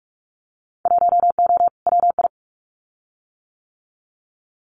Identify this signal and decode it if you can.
Morse code, heard as a keyed tone, about 44 wpm, 700 Hz. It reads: 1O WI
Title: Winter Field Day Exchanges 45wpm